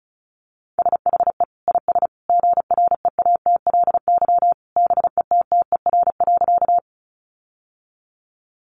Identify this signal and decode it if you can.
Morse code, heard as a keyed tone, about 34 wpm, 700 Hz. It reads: SHE IS GREATLY BETTER.
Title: Sentences from Top 1400 Words 35wpm